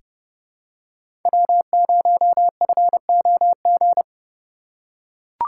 Morse code, heard as a keyed tone, about 32 wpm, 700 Hz. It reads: W0FOG
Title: USA Call Signs - 5 Characters - 2x Character Spacing - 30wpm